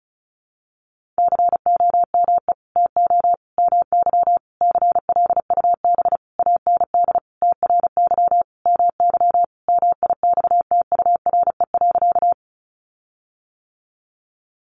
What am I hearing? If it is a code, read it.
Morse code, heard as a keyed tone, about 34 wpm, 700 Hz. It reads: COME TO MY CLUB AND TRY MY MIXTURE.
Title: Sentences from Top 1500 Words 35wpm